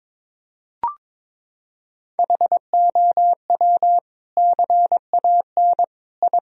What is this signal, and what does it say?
Morse code, heard as a keyed tone, about 21 wpm, 700 Hz. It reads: HOW CAN I
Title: Sentences from Top 1000 Words - Review Part 4 - 22wpm